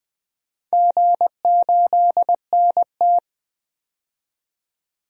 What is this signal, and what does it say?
Morse code, heard as a keyed tone, about 20 wpm, 700 Hz. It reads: G8NT
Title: DXCC-100 Call Signs - 4 Characters - 20wpm